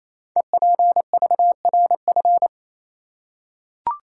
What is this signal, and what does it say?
Morse code, heard as a keyed tone, about 28 wpm, 700 Hz. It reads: EPVRF
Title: Code Groups - 5 Letters - 28wpm